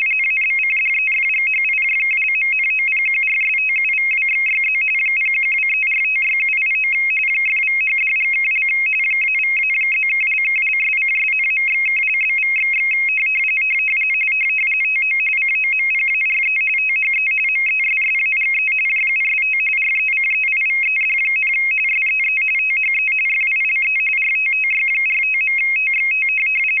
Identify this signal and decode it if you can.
RTTY, about 45.45 baud. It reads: ABCDEFGHIJKLMNOPQRSTUVWXYZ 0123456789 !=#$&() -=-+()()':',./()?=// TO COMPARE THE MESSAGE JUST RECEIVED, SEE THE FILE RTTY45.TXT
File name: rtty425 / rtty45